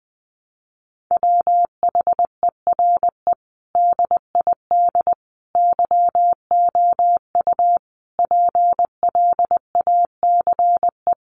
Morse code, heard as a keyed tone, about 19 wpm, 700 Hz. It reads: WHERE DID YOU PLACE
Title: Sentences from Top 200 Words 20wpm